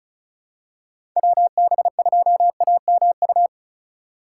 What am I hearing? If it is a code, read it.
Morse code, heard as a keyed tone, about 36 wpm, 700 Hz. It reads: WB2AMU
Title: USA Call Signs - 6 Characters - 4x Character Spacing - 35wpm